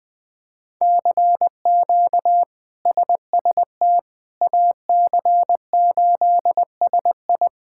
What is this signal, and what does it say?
Morse code, heard as a keyed tone, about 20 wpm, 700 Hz. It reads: CQ SST AC8SI
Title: K1USN SST QSO Practice - 20wpm